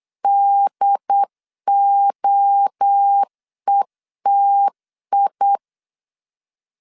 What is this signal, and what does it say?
Morse code, heard as a keyed tone, about 8 wpm, 800 Hz. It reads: DOETI
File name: CW / DET / 1 3 1500C S A 8wpm 4229